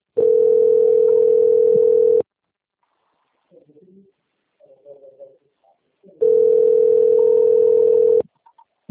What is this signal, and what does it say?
Ringback tone (North America), 440 + 480 Hz.